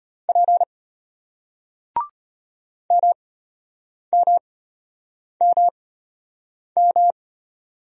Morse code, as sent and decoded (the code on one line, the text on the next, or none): .--. / -- -- -- --
P MMMM